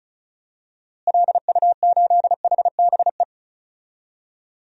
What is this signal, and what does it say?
Morse code, heard as a keyed tone, about 36 wpm, 700 Hz. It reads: LU8HBE
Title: DXCC-100 Call Signs - 6 Characters - Repeated 3x - 35wpm